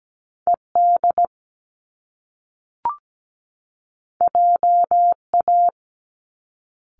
Morse code, heard as a keyed tone, about 17 wpm, 700 Hz. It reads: ED JA
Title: Top Two Letter Combinations in Words 17wpm